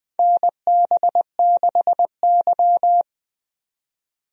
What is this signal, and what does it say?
Morse code, heard as a keyed tone, about 20 wpm, 700 Hz. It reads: NB6Y
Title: USA Call Signs - 4 Characters - 20wpm